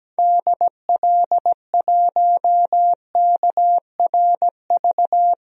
Morse code, heard as a keyed tone, about 17 wpm, 700 Hz. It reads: DL1KRV
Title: DXCC-100 Call Signs - 6 Characters - 17wpm